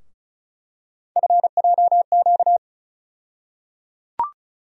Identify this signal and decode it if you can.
Morse code, heard as a keyed tone, about 34 wpm, 700 Hz. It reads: FJQ
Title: USA Call Sign Postfixes - 3 Letters 35wpm